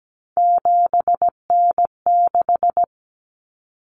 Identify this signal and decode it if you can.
Morse code, heard as a keyed tone, about 17 wpm, 690 Hz. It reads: 7N6
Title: Common Cut Numbers - 3-Digit 17wpm